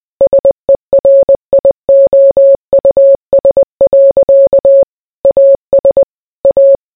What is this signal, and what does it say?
Morse code, heard as a keyed tone, about 20 wpm, 550 Hz. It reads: SERIOUS. AS A